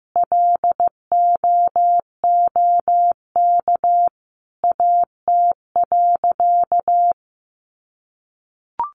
Morse code, heard as a keyed tone, about 15 wpm, 700 Hz. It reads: LOOK AT.
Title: Sentences from Top 200 Words 15wpm